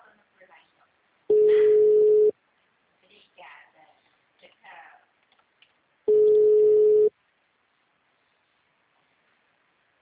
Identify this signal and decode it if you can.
Ringback tone (Europe, 425 Hz), with speech.